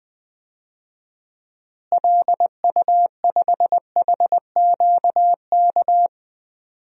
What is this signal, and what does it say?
Morse code, heard as a keyed tone, about 21 wpm, 700 Hz. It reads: LU5HQK
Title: DXCC-100 Call Signs - 6 Characters - Repeated 2x - 20wpm